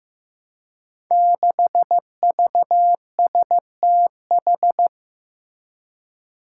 Morse code, heard as a keyed tone, about 15 wpm, 700 Hz. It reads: 6VSTH
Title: Five Challenging Letter-Number Combinations 15wpm